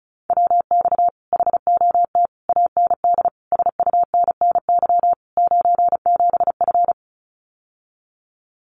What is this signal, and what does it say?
Morse code, heard as a keyed tone, about 34 wpm, 700 Hz. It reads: WX HOT AND SUNNY 97F
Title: Common QSO Phrases Encore 35wpm